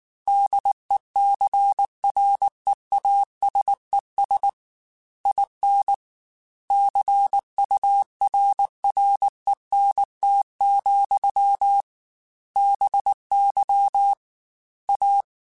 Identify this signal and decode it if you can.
Morse code, heard as a keyed tone, about 19 wpm, 800 Hz. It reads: DECREASES IN CURRENT, BY A